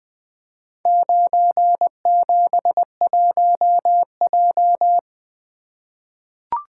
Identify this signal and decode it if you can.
Morse code, heard as a keyed tone, about 20 wpm, 700 Hz. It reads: 971J
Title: Four Letter-Number Combinations - 20wpm